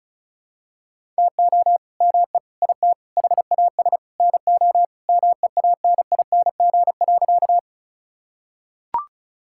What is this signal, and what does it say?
Morse code, heard as a keyed tone, about 36 wpm, 700 Hz. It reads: TO ME IT HAS NO MEANING.
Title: Sentences from Top 1000 Words - Review Part 6 - 35wpm